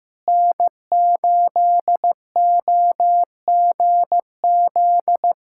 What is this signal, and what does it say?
Morse code, heard as a keyed tone, about 15 wpm, 700 Hz. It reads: N8OGZ
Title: USA Callsign DE Callsign - 5 Characters - 15wpm